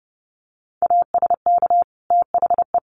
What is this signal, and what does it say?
Morse code, heard as a keyed tone, about 30 wpm, 700 Hz. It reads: ASK THE
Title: Sentences from Top 1500 Words 30wpm